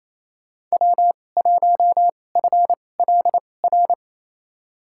Morse code, heard as a keyed tone, about 28 wpm, spaced as for 20 wpm, 700 Hz. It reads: W1FLR
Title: USA Call Signs - 2x Character Spacing - 28wpm